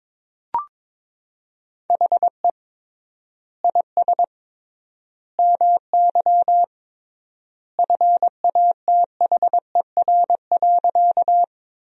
Morse code, heard as a keyed tone, about 21 wpm, 700 Hz. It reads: HE IS MY FATHER.
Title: 3x Word-Spacing - Sentences from Top 300 Words - 22wpm